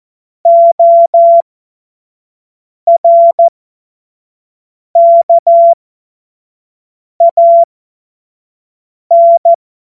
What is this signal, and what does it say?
Morse code, heard as a keyed tone, about 14 wpm, spaced as for 5 wpm, 680 Hz. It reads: ORKAN
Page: cw-witze